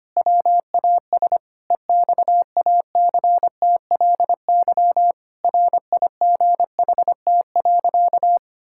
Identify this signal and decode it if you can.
Morse code, heard as a keyed tone, about 25 wpm, 700 Hz. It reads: WAS EXACTLY RIGHT.
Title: Sentences from Top 1000 Words - Review Part 6 - 25wpm